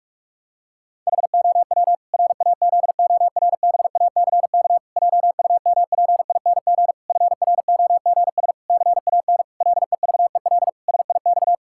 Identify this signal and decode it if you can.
Morse code, heard as a keyed tone, about 46 wpm, 700 Hz. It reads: HOW RAZORBACK JUMPING FROGS CAN LEVEL SIX
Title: Pangram Challenge v2 - 45wpm